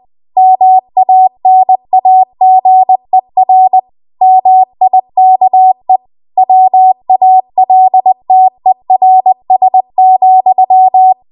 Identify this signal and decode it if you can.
Morse code, heard as a keyed tone, about 20 wpm, 750 Hz. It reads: MANAGER MIKE WALTERS,
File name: Archive / 20 WPM / 20WPM